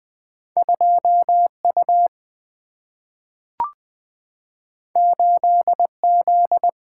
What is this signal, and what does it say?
Morse code, heard as a keyed tone, about 20 wpm, 700 Hz. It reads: 2U 8Z
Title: Single-Letter Single-Number 20wpm